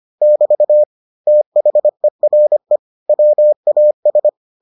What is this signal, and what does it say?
Morse code, heard as a keyed tone, about 25 wpm, 600 Hz. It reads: <BT> THERE WAS